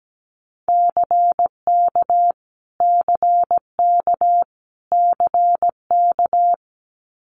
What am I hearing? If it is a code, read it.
Morse code, heard as a keyed tone, about 17 wpm, 700 Hz. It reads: CK CK CK
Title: Multi-letter Phonemes Repeated 3x 17wpm